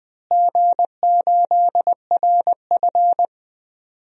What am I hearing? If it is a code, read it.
Morse code, heard as a keyed tone, about 20 wpm, 700 Hz. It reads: G8RF
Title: DXCC-100 Call Signs - 4 Characters - Repeated 3x - 20wpm